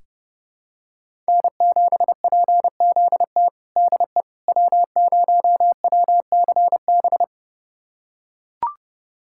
Morse code, heard as a keyed tone, about 32 wpm, 700 Hz. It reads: N7PZT DE W0WCB
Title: USA Callsign DE Callsign 30wpm